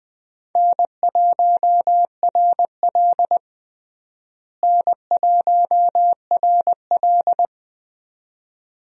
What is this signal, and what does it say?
Morse code, heard as a keyed tone, about 21 wpm, 700 Hz. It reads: N1RL N1RL
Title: USA Call Signs - 4 Characters - Repeated 2x - 20wpm